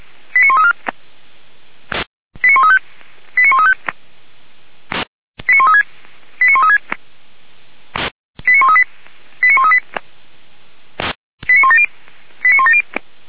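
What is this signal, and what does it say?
Five-tone selective call (ZVEI), heard as quick runs of five tones, tones of about 70 ms. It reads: 89134 89135 89135 89136 89136 89138 89138 89170 89170
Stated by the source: ZVEI five-tone selective call received on an SDR by Feuerwehr Orlamünde.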